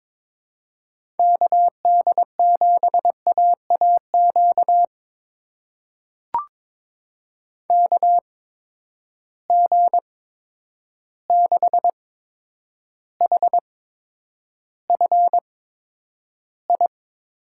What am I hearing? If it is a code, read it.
Morse code, heard as a keyed tone, about 22 wpm, spaced as for 9 wpm, 700 Hz. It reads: KD7AAQ KG6HFI
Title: USA Call Signs - 6 Characters - 8x Character Spacing - 22wpm